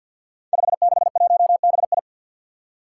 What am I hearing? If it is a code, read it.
Morse code, heard as a keyed tone, about 47 wpm, 700 Hz. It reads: 561BI
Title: Five Challenging Letter-Number Combinations 50wpm